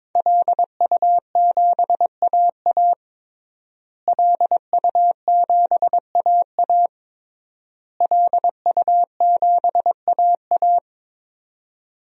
Morse code, heard as a keyed tone, about 22 wpm, 700 Hz. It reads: LU7AA LU7AA LU7AA
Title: DXCC-100 Call Signs - 5 Characters - Repeated 3x - 22wpm